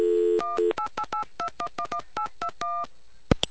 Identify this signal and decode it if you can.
DTMF, heard as telephone touch tones, tones of about 90 ms. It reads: 18882111821